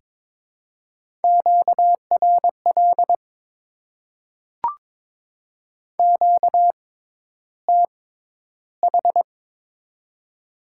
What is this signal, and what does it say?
Morse code, heard as a keyed tone, about 22 wpm, spaced as for 11 wpm, 700 Hz. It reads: QRL QTH
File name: Top 20 QSO Elements - 6x Character Spacing - 22wpm